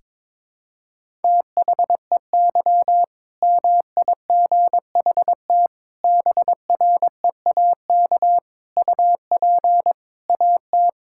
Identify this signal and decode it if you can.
Morse code, heard as a keyed tone, about 22 wpm, 700 Hz. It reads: THEY MIGHT BREAK UP AT
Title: Sentences from Top 1000 Words - Review Part 6 - 22wpm